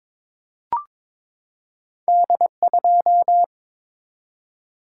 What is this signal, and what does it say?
Morse code, heard as a keyed tone, about 22 wpm, 700 Hz. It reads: D2
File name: Two Letter-Number Combinations - 22wpm